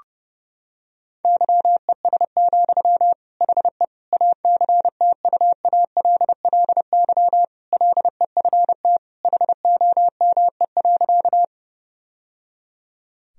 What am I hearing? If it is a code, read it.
Morse code, heard as a keyed tone, about 32 wpm, 700 Hz. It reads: YES, HE ACTUALLY LEFT HOME.
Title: Sentences from Top 1000 Words - Review Part 4 - 30wpm